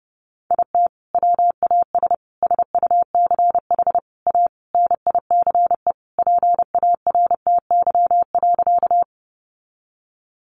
Morse code, heard as a keyed tone, about 28 wpm, 700 Hz. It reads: IT WAS SUCH A NICE PARTY.
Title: Sentences from Top 1400 Words 30wpm